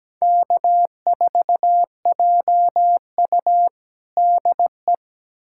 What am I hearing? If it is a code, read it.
Morse code, heard as a keyed tone, about 17 wpm, 700 Hz. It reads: K4JU DE
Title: USA Callsign DE Callsign - 4 Characters - 17wpm